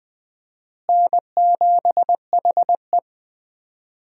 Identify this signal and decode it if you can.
Morse code, heard as a keyed tone, about 20 wpm, 700 Hz. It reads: N7HE
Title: USA Call Signs - 4 Characters - Repeated 2x - 20wpm